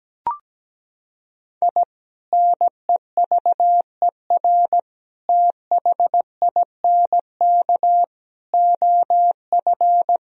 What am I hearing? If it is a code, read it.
Morse code, heard as a keyed tone, about 17 wpm, 700 Hz. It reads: I NEVER THINK OF
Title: Sentences from Top 1000 Words - Review Part 1 - 17wpm